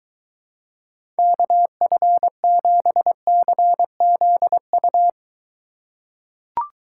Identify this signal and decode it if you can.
Morse code, heard as a keyed tone, about 23 wpm, 700 Hz. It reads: KF7CZU